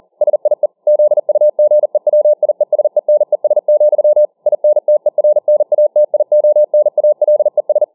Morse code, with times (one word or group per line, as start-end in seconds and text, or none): 0.2-0.7 SIE
0.9-4.2 ZUGEWIESENES,
4.5-7.8 INTERNATIONALES